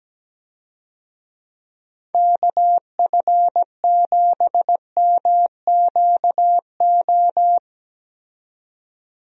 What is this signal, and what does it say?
Morse code, heard as a keyed tone, about 17 wpm, 700 Hz. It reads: KF7MQO
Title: USA Call Signs - 6 Characters - 17wpm